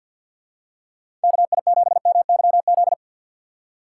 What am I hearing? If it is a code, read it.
Morse code, heard as a keyed tone, about 52 wpm, 700 Hz. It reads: KI7MYZ
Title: USA Call Signs - 3x Character Spacing - 50wpm